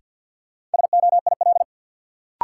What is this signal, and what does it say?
Morse code, heard as a keyed tone, about 46 wpm, 700 Hz. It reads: SOIL
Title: Sentences from Top 1500 Words - New Word Drill Alt - 50wpm